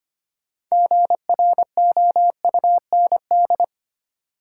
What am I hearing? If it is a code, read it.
Morse code, heard as a keyed tone, about 25 wpm, 700 Hz. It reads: GROUND
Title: Sets of 3 Words - New Word Drill - 25wpm